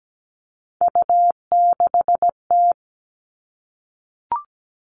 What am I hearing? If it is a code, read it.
Morse code, heard as a keyed tone, about 17 wpm, 700 Hz. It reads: U6T
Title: All Cut Numbers - 3-Digit 17wpm